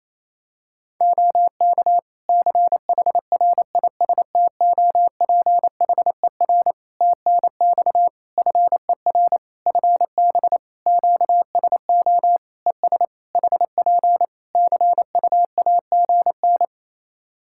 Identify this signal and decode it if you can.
Morse code, heard as a keyed tone, about 27 wpm, 700 Hz. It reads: OK CHRISTOPHER TNX FER FB QSO ES HP CUAGN
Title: Practice QSO - Exchange 4 - Ending - 28wpm